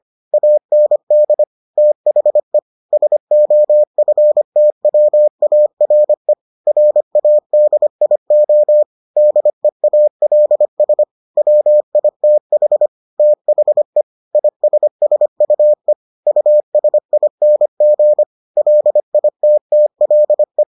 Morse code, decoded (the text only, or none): AND THE SOFTWARE RADIO DEALS WITH THE ISSUE USING LITTLE